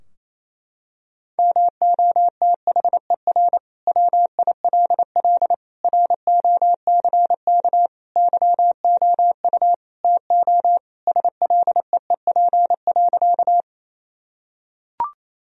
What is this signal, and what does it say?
Morse code, heard as a keyed tone, about 28 wpm, 700 Hz. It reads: MOTHER WILL ROCK YOU TO SLEEP.
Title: Sentences from Top 1000 Words - Review Part 2 - 28wpm